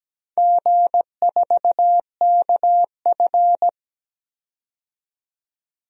Morse code, heard as a keyed tone, about 17 wpm, 700 Hz. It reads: G4KF
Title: DXCC-100 Call Signs - 4 Characters - 17wpm